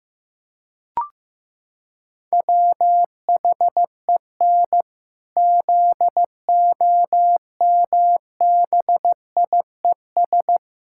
Morse code, heard as a keyed tone, about 15 wpm, 700 Hz. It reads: WHEN ZOMBIES
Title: Pangram Challenge v2 - 15wpm